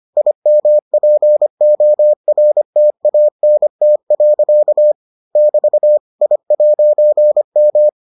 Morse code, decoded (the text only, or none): IMPORTANT. <BT> I'M